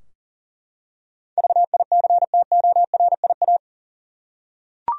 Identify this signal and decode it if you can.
Morse code, heard as a keyed tone, about 37 wpm, 700 Hz. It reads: VICTORIA
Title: USA Common Names - 40wpm